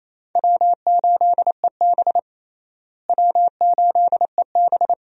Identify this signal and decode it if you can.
Morse code, heard as a keyed tone, about 28 wpm, 700 Hz. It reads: W8EB W8EB
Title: USA Call Signs - 4 Characters - Repeated 2x - 28wpm